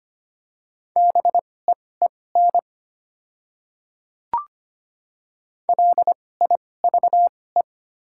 Morse code, heard as a keyed tone, about 25 wpm, spaced as for 18 wpm, 700 Hz. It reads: BEEN LIVE
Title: 2x Character Spacing - Top 300 Words - 25wpm